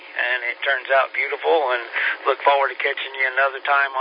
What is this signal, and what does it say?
Emergency Alert System (SAME) tones, heard as harsh buzzing bursts, with speech. It is heard from 0.4 to 3.5 s.